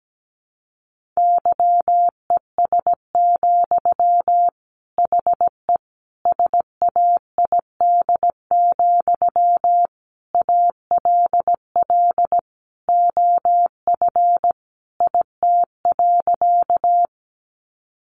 Morse code, heard as a keyed tone, about 17 wpm, 700 Hz. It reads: YES, HE SAID, ALL OF IT.
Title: Sentences from Top 500 Words 17wpm